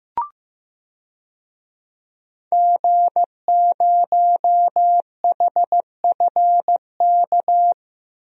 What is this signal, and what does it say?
Morse code, heard as a keyed tone, about 16 wpm, 700 Hz. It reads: G0HFK
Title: DXCC-100 Call Signs - 5 Characters - Repeated 3x - 15wpm